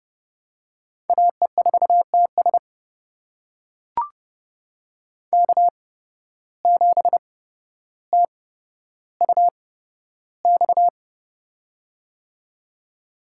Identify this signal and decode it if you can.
Morse code, heard as a keyed tone, about 30 wpm, spaced as for 13 wpm, 700 Hz. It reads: AE4TS K7TUX
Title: USA Call Signs - 5 Characters - 8x Character Spacing - 30wpm